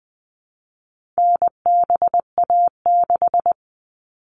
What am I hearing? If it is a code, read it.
Morse code, heard as a keyed tone, about 20 wpm, 700 Hz. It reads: NBA6
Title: All Cut Numbers - 4-Digit 20wpm